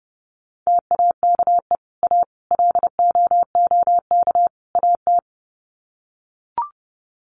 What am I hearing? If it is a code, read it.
Morse code, heard as a keyed tone, about 30 wpm, 700 Hz. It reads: TAKE A LOOK AT
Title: Sets of 4 Words 30wpm